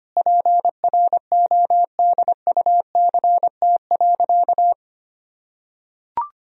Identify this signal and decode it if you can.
Morse code, heard as a keyed tone, about 25 wpm, 700 Hz. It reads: PRODUCT.
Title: Sentences from Top 1000 Words - Review Part 3 - 25wpm